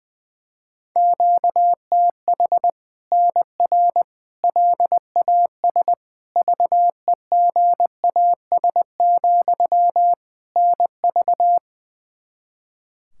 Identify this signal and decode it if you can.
Morse code, heard as a keyed tone, about 20 wpm, 700 Hz. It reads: QTH NR LAS VEGAS, NV
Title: Practice QSO - Exchange 1 - Beginning - 20wpm